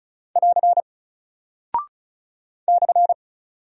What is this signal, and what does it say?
Morse code, heard as a keyed tone, about 36 wpm, 700 Hz. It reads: <AR> /
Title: Single Character (Letter:Number:Punctuation:Prosign) Alt 35wpm